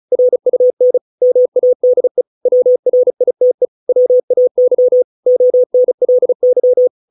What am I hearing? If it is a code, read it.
Morse code, heard as a keyed tone, about 34 wpm, 500 Hz. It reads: RUN MADE WRITE WAY ONLY
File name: 632 words / 632 Words 35WPM